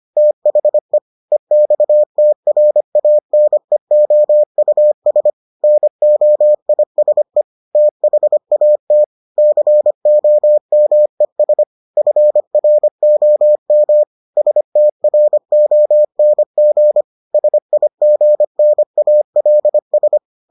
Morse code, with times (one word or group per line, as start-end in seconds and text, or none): 0.2-1.0 THE
1.3-5.3 EXTRANEOUS
5.6-7.4 NOISE
7.8-9.0 THAT
9.4-11.6 COMES
12.0-14.0 FROM
14.4-17.0 STRONG
17.3-20.2 SIGNALS